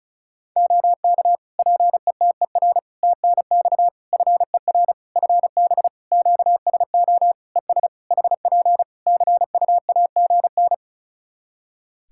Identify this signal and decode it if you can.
Morse code, heard as a keyed tone, about 36 wpm, 700 Hz. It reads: OK PETER TNX FER FB QSO ES HP CUAGN
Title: Practice QSO - Exchange 4 - Ending - 35wpm